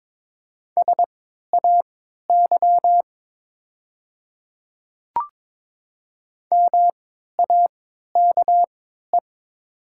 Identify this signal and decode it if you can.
Morse code, heard as a keyed tone, about 22 wpm, spaced as for 12 wpm, 700 Hz. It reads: SAY MAKE
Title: 3x Character Spacing - Top 200 Words - 22wpm